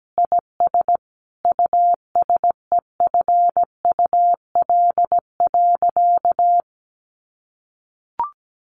Morse code, heard as a keyed tone, about 17 wpm, 700 Hz. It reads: IS USEFUL.
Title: Sentences from Top 1500 Words 17wpm